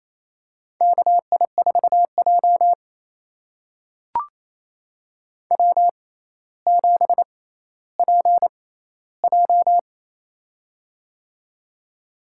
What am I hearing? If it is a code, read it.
Morse code, heard as a keyed tone, about 28 wpm, spaced as for 14 wpm, 700 Hz. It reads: KI4J W7PJ
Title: USA Call Signs - 4 Characters - 6x Character Spacing - 28wpm